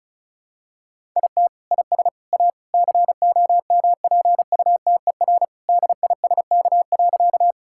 Morse code, heard as a keyed tone, about 34 wpm, 700 Hz. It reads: IT IS A COMPUTER DISK.
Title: Sentences from Top 1000 Words - Review Part 6 - 35wpm